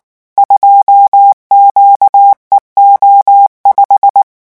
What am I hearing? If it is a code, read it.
Morse code, heard as a keyed tone, about 19 wpm, 800 Hz. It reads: 2QEO5
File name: morse mp3 / revision17-vitesse 18 mots